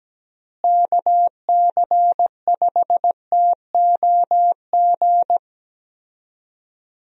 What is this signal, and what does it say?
Morse code, heard as a keyed tone, about 17 wpm, 700 Hz. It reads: KC5TOG
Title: USA Call Signs - 6 Characters - Repeated 2x - 17wpm